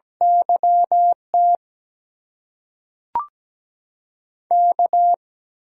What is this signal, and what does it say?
Morse code, heard as a keyed tone, about 17 wpm, spaced as for 5 wpm, 700 Hz. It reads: YTK
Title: USA Call Signs - 8x Character Spacing - 17wpm